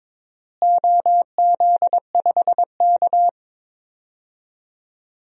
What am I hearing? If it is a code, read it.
Morse code, heard as a keyed tone, about 21 wpm, 700 Hz. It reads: OZ5K
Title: DXCC-100 Call Signs - 4 Characters - Repeated 2x - 22wpm